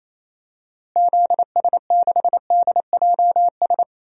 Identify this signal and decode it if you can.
Morse code, heard as a keyed tone, about 28 wpm, 700 Hz. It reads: ZS6DJS
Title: DXCC-100 Call Signs - 6 Characters - Repeated 2x - 28wpm